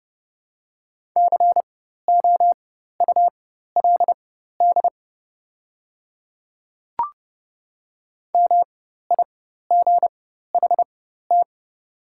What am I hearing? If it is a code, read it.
Morse code, heard as a keyed tone, about 30 wpm, spaced as for 14 wpm, 700 Hz. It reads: COULD MIGHT
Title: 4x Character Spacing - Top 500 Words - 30wpm